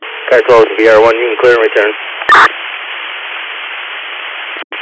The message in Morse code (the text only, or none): none